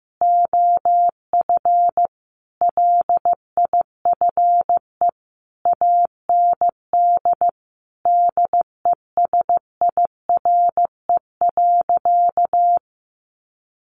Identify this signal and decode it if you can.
Morse code, heard as a keyed tone, about 15 wpm, 700 Hz. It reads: OF LIFE AND DESIRE.
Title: Sentences from Top 1100 Words 15wpm